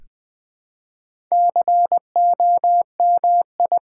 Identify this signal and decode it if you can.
Morse code, heard as a keyed tone, about 20 wpm, 700 Hz. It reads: COMI
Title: Build A Word - 1 Letter At A Time - 3 to 5-Letter Words - 20wpm